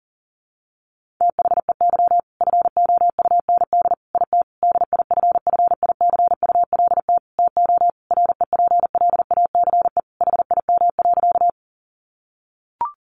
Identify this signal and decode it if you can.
Morse code, heard as a keyed tone, about 40 wpm, 700 Hz. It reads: THEY FOUND IT DIFFICULT TO REPLACE HIM.
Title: Sentences from Top 1400 Words 40wpm